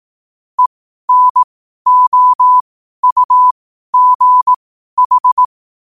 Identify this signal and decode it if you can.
Morse code, heard as a keyed tone, about 18 wpm, spaced as for 12 wpm, 1000 Hz. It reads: ENOUGH
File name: qotdfiles / 12wpm